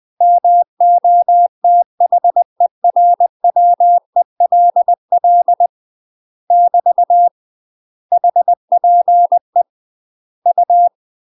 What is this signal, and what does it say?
Morse code, heard as a keyed tone, about 20 wpm, 700 Hz. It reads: MOTHERWELL <BT> HPE U
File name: UK MASTER QSOs TRAINING / 20WPM / UK MASTER QSOs 0059